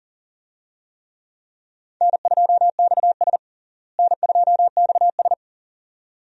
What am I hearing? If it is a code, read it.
Morse code, heard as a keyed tone, about 37 wpm, 700 Hz. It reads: N2XS N2XS
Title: USA Call Signs - 4 Characters - Repeated 2x - 40wpm